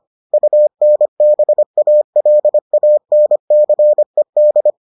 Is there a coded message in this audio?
Yes: Morse code from 0.3 to 4.7 s, heard as a keyed tone, about 25 wpm, 600 Hz. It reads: UNBALANCED